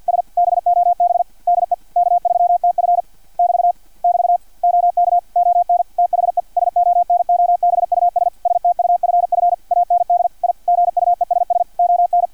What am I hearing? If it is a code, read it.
Morse code, heard as a keyed tone, about 50 wpm, 700 Hz, with radio noise. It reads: S BOB DE K3TF <BT> <BT> OK ON THE SONOBUS STUFF AND I GUESS ON
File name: archived / CRSnet